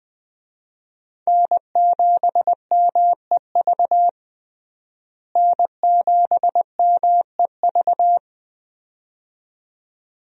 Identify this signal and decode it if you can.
Morse code, heard as a keyed tone, about 20 wpm, 700 Hz. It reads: N7MEV N7MEV
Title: USA Call Signs - 5 Characters - Repeated 2x - 20wpm